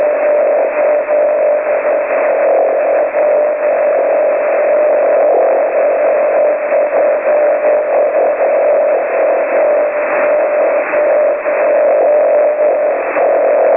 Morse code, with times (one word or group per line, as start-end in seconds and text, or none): none